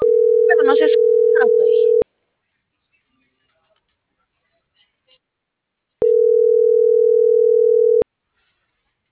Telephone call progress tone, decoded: ringback tone

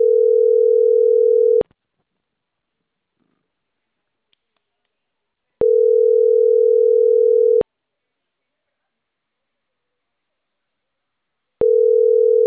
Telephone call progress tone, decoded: ringback tone

